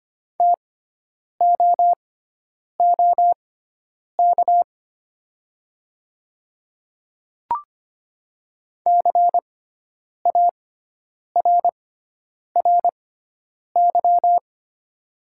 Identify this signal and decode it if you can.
Morse code, heard as a keyed tone, about 25 wpm, spaced as for 9 wpm, 700 Hz. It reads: TOOK CARRY
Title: 6x Character Spacing - Top 300 Words - 25wpm